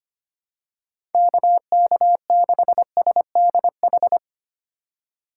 Morse code, heard as a keyed tone, about 25 wpm, 700 Hz. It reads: KK6SDH
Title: USA Call Signs - 6 Characters - 8x Character Spacing - 25wpm